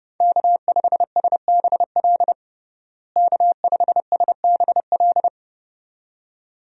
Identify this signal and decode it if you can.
Morse code, heard as a keyed tone, about 30 wpm, 700 Hz. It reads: K5SBL K5SBL
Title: USA Call Signs - 5 Characters - Repeated 2x - 30wpm